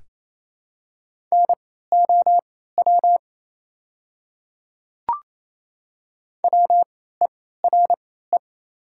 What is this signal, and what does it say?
Morse code, heard as a keyed tone, about 28 wpm, spaced as for 16 wpm, 700 Hz. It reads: NOW WERE
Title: 3x Character Spacing - Top 100 Words - 28wpm